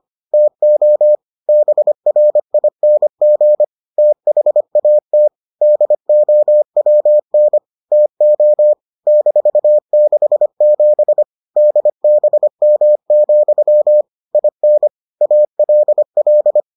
Morse code, heard as a keyed tone, about 25 wpm, 600 Hz. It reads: TO BRING THAT DOWN TO -67 DBM, IN ALL